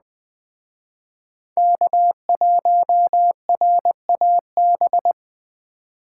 Morse code, heard as a keyed tone, about 20 wpm, 700 Hz. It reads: K1RAB
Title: Speed Racing - USA 1x3 Call Signs 17wpm